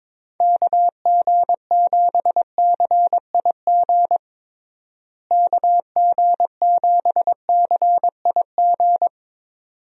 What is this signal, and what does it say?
Morse code, heard as a keyed tone, about 22 wpm, 700 Hz. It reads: KG7CIG KG7CIG
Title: USA Call Signs - 6 Characters - Repeated 2x - 22wpm